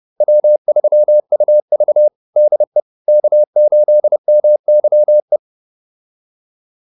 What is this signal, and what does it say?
Morse code, heard as a keyed tone, about 30 wpm, 600 Hz. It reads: W3UV DE K8MYE